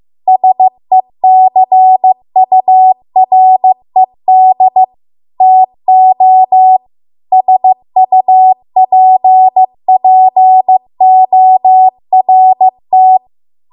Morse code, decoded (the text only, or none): SECURED TO SUPPORT